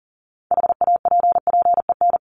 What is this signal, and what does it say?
Morse code, heard as a keyed tone, about 40 wpm, 700 Hz. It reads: HAPPEN
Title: ED Suffix - Top 100 Words with Base Words 40wpm